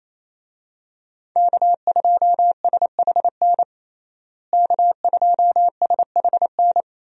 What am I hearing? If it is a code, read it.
Morse code, heard as a keyed tone, about 28 wpm, 700 Hz. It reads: K2SHN K2SHN
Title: USA Call Signs - 5 Characters - Repeated 2x - 28wpm